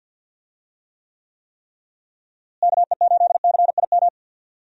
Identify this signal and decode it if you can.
Morse code, heard as a keyed tone, about 46 wpm, 700 Hz. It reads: KE8CIM
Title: USA Call Signs - 6 Characters - Repeated 3x - 50wpm